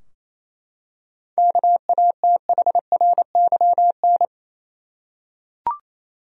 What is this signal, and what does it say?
Morse code, heard as a keyed tone, about 28 wpm, 700 Hz. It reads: KATHRYN
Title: USA Common Names - 28wpm